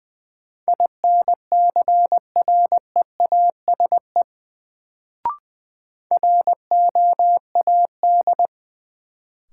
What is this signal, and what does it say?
Morse code, heard as a keyed tone, about 20 wpm, 700 Hz. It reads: INCREASE ROAD
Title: Sets of 5 Words - New Word Drill - 20wpm